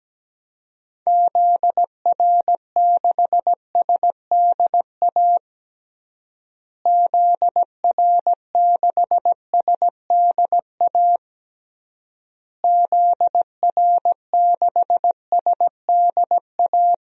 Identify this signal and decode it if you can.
Morse code, heard as a keyed tone, about 17 wpm, 700 Hz. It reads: ZR6SDA ZR6SDA ZR6SDA
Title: DXCC-100 Call Signs - 6 Characters - Repeated 3x - 17wpm